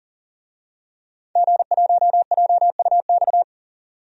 Morse code, heard as a keyed tone, about 40 wpm, 700 Hz. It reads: G1JUX
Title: DXCC-100 Call Signs - 5 Characters - 40wpm